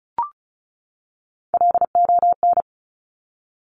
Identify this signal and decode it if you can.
Morse code, heard as a keyed tone, about 36 wpm, 700 Hz. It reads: LON
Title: Top Three Letter Combinations in Words 35wpm